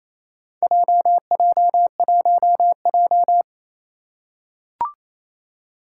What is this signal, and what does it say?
Morse code, heard as a keyed tone, about 28 wpm, 700 Hz. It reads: JJ1J
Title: Four Challenging Letter-Number Combinations 28wpm